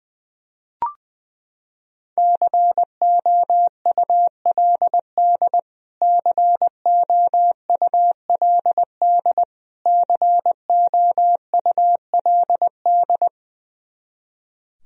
Morse code, heard as a keyed tone, about 20 wpm, 700 Hz. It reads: COULD COULD COULD
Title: Top 100 Words Repeated 20wpm